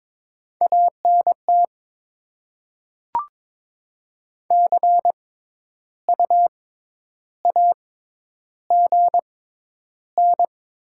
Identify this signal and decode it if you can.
Morse code, heard as a keyed tone, about 21 wpm, spaced as for 10 wpm, 700 Hz. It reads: ANT CU AGN
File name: Top 20 QSO Elements - 6x Character Spacing - 22wpm